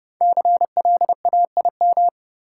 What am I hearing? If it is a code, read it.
Morse code, heard as a keyed tone, about 30 wpm, 700 Hz. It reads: CLAIM
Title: Five Letter Words - 30wpm